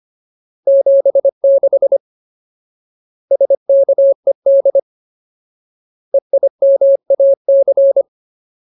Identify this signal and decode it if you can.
Morse code, heard as a keyed tone, about 25 wpm, 550 Hz. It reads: 76 SKED EIMAC
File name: Ultimate Word List / 25WPM Ultimate